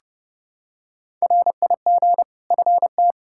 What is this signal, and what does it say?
Morse code, heard as a keyed tone, about 32 wpm, 700 Hz. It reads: RIG FT
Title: Practice QSO - Exchange 2 - Info - 30wpm